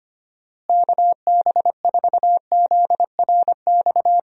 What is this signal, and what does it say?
Morse code, heard as a keyed tone, about 25 wpm, 700 Hz. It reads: KB4ZRX
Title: USA Call Signs - 6 Characters - Repeated 2x - 25wpm